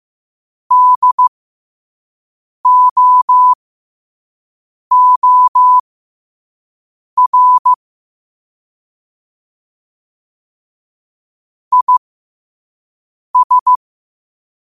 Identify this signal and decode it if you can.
Morse code, heard as a keyed tone, about 15 wpm, spaced as for 5 wpm, 1000 Hz. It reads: DOOR IS
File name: qotdfiles / 5wpm